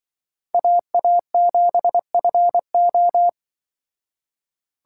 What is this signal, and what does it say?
Morse code, heard as a keyed tone, about 24 wpm, 700 Hz. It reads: AA7FO
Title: Speed Racing - USA 2x2 Call Signs 20wpm